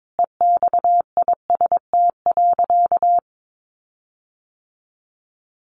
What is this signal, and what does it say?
Morse code, heard as a keyed tone, about 22 wpm, 700 Hz. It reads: EXIST.
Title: Sentences from Top 1400 Words 22wpm